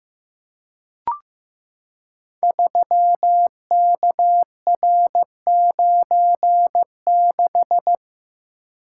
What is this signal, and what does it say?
Morse code, heard as a keyed tone, about 15 wpm, 700 Hz. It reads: 3KR96
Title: Code Groups - 5 Letter:Numbers - 15wpm